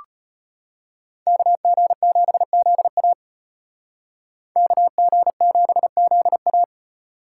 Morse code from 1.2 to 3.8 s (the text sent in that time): KG7ZA